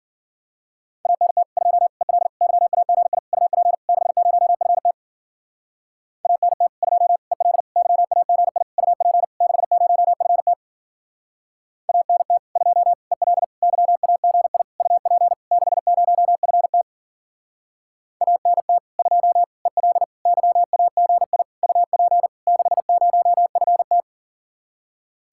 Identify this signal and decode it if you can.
Morse code, heard as a keyed tone, about 48 wpm, 700 Hz. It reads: ANT 2 EL YAGI UP 60FT ANT 2 EL YAGI UP 60FT ANT 2 EL YAGI UP 60FT ANT 2 EL YAGI UP 60FT
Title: Speed-Racing Common QSO Phrases Encore - 40wpm